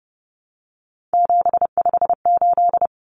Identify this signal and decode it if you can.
Morse code, heard as a keyed tone, about 30 wpm, 700 Hz. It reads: 758
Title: Common Cut Numbers - 3-Digit 30wpm